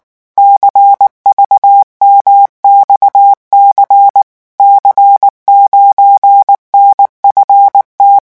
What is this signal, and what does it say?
Morse code, heard as a keyed tone, about 19 wpm, 800 Hz. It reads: CVMXC C9NFT
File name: morse mp3 / revision13-vitesse 18 mots